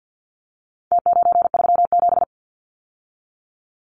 Morse code, heard as a keyed tone, about 48 wpm, 700 Hz. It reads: T937
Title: Numbers - Contest Style - 4-Digit 50wpm